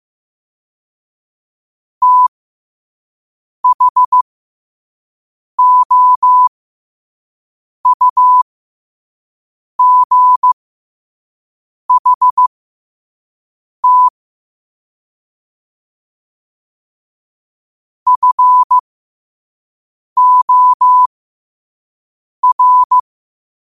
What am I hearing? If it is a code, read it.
Morse code, heard as a keyed tone, about 15 wpm, spaced as for 5 wpm, 1000 Hz. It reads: THOUGHT FOR
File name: qotdfiles / 5wpm